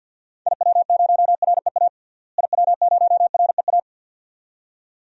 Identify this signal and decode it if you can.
Morse code, heard as a keyed tone, about 50 wpm, 700 Hz. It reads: IW0REA IW0REA
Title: DXCC-100 Call Signs - 6 Characters - Repeated 2x - 50wpm